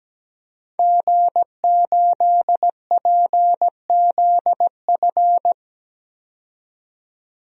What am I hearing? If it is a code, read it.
Morse code, heard as a keyed tone, about 17 wpm, 700 Hz. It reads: G8PZF
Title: DXCC-100 Call Signs - 5 Characters - 17wpm